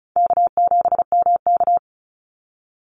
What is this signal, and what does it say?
Morse code, heard as a keyed tone, about 34 wpm, 700 Hz. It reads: K7MK
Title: NA SOTA Activator Callsigns 35wpm